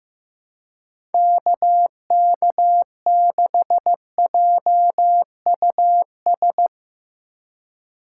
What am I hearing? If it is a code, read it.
Morse code, heard as a keyed tone, about 15 wpm, 700 Hz. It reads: KK6JUS
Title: USA Call Signs - 4x Character Spacing - 15wpm